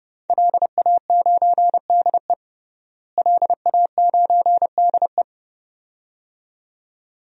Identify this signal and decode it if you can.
Morse code, heard as a keyed tone, about 30 wpm, 700 Hz. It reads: LA9DE LA9DE
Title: DXCC-100 Call Signs - 5 Characters - Repeated 2x - 30wpm